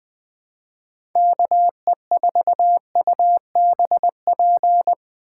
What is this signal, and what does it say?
Morse code, heard as a keyed tone, about 20 wpm, 700 Hz. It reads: KE4UBP